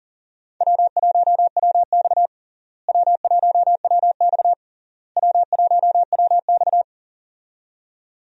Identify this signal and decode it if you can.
Morse code, heard as a keyed tone, about 40 wpm, 700 Hz. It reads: W1WX W1WX W1WX
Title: USA Call Signs - 4 Characters - Repeated 3x - 40wpm